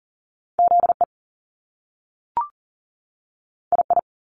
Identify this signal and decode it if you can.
Morse code, heard as a keyed tone, about 38 wpm, 700 Hz. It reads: ZE II
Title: Top Two Letter Combinations in Words 40wpm